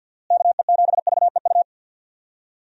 Morse code, heard as a keyed tone, about 51 wpm, 700 Hz. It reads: KE7VEU
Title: USA Call Signs - 6 Characters - Repeated 2x - 50wpm